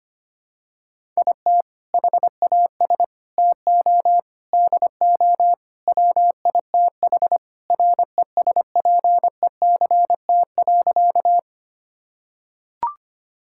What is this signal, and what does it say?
Morse code, heard as a keyed tone, about 25 wpm, 700 Hz. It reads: IT HAS TO DO WITH RESPECT.
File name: practice / 25wpm / Sentences with 4-Word Phrases - 25wpm